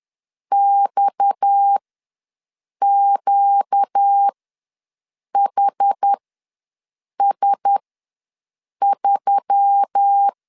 Morse code, heard as a keyed tone, about 10 wpm, spaced as for 6 wpm, 800 Hz. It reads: XQHS3
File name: LEZIONI / 6 PKX0 / 6 PKX ws 10wpm 3706